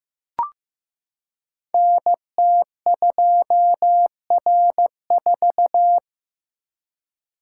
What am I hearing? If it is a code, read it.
Morse code, heard as a keyed tone, about 15 wpm, 700 Hz. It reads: NT2R4
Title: Code Groups - 5 Letter:Numbers - 15wpm